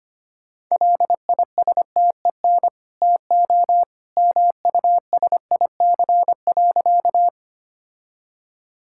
Morse code, heard as a keyed tone, about 25 wpm, 690 Hz. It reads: LISTEN TO MUSIC.
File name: Easy Sentences with Hints - 25wpm